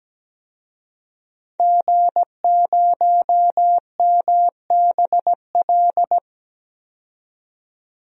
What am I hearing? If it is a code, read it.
Morse code, heard as a keyed tone, about 17 wpm, 700 Hz. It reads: G0MBL
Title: DXCC-100 Call Signs - 5 Characters - 17wpm